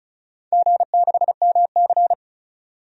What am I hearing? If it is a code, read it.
Morse code, heard as a keyed tone, about 36 wpm, 690 Hz. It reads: G6MC
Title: DXCC-100 Call Signs - 4 Characters - 35wpm